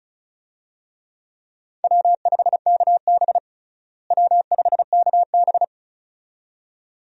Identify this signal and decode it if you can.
Morse code, heard as a keyed tone, about 36 wpm, 700 Hz. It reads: W5KB W5KB
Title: USA Call Signs - 4 Characters - Repeated 2x - 35wpm